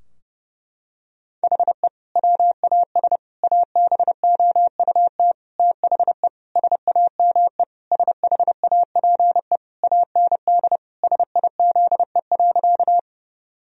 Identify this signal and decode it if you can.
Morse code, heard as a keyed tone, about 32 wpm, 700 Hz. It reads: HE WAS ABOUT THE SAME SHAPE AND SIZE.
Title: Sentences from Top 1000 Words - Review Part 3 - 30wpm